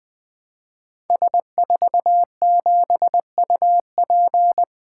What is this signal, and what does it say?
Morse code, heard as a keyed tone, about 20 wpm, 700 Hz. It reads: S47UP
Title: Code Groups - 5 Letter:Numbers - 20wpm